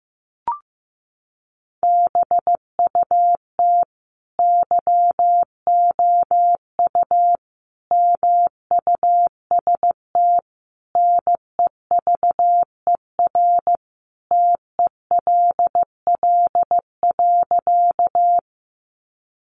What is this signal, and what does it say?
Morse code, heard as a keyed tone, about 15 wpm, 700 Hz. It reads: BUT YOU MUST NEVER TELL.
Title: Sentences from Top 300 Words 15wpm